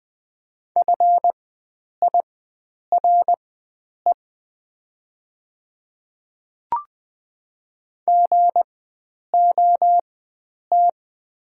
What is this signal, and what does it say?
Morse code, heard as a keyed tone, about 20 wpm, spaced as for 9 wpm, 700 Hz. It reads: FIRE GOT
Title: 4x Character Spacing - Top 500 Words - 20wpm